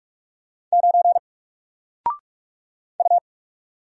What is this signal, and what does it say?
Morse code, heard as a keyed tone, about 44 wpm, 700 Hz. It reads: 9 U
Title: Single Letter-Number Alt 45wpm